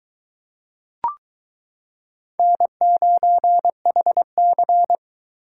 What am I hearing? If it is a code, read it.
Morse code, heard as a keyed tone, about 23 wpm, 700 Hz. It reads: N9HC